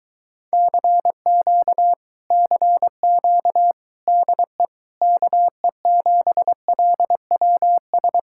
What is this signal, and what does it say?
Morse code, heard as a keyed tone, about 23 wpm, 700 Hz. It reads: CQ CQ DE KE7LWS